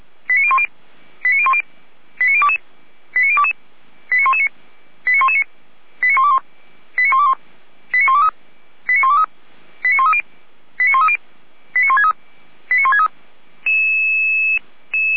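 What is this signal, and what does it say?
Five-tone selective call (ZVEI), heard as quick runs of five tones, tones of about 70 ms. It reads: 89010 89010 89022 89022 89119 89119 89121 89121 89123 89123 89130 89130 89163 89163
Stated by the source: ZVEI five-tone call received on SDR.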